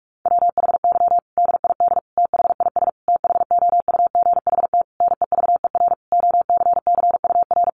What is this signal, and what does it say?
Morse code, heard as a keyed tone, about 44 wpm, 700 Hz. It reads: WHY DID THIS THOUGHT NEVER OCCUR
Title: Sentences from Top 900 Words 45wpm